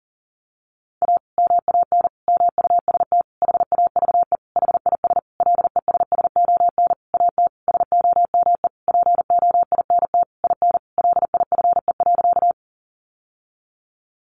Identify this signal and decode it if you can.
Morse code, heard as a keyed tone, about 40 wpm, 700 Hz. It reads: A MAN MUST HAVE HIS LESSON AT SOME POINT IN LIFE.
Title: Sentences from Top 1200 Words 40wpm